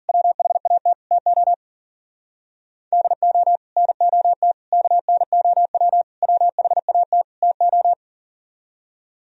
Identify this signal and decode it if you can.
Morse code, heard as a keyed tone, about 51 wpm, 700 Hz. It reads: WHAT TO DO NOT KNOW WHAT TO